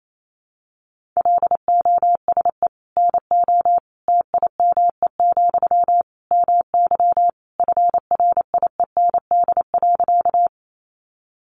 Morse code, heard as a keyed tone, about 28 wpm, 700 Hz. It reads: LOSE NO TIME, MY FRIEND.
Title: Sentences from Top 1100 Words 28wpm